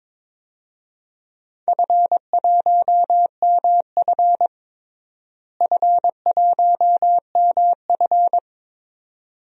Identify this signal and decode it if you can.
Morse code, heard as a keyed tone, about 22 wpm, 700 Hz. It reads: F1MF F1MF
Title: DXCC-100 Call Signs - 4 Characters - Repeated 2x - 22wpm